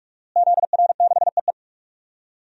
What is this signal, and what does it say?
Morse code, heard as a keyed tone, about 46 wpm, 700 Hz. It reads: ZR6EE
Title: DXCC-100 Call Signs - 5 Characters - Repeated 2x - 45wpm